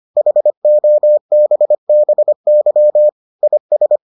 Morse code, heard as a keyed tone, about 25 wpm, 600 Hz. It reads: HOBBY IS